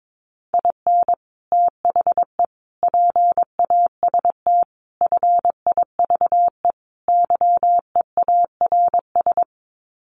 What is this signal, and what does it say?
Morse code, heard as a keyed tone, about 22 wpm, 700 Hz. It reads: IN THE PAST FIVE YEARS
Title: Sets of 5 Words Encore 22wpm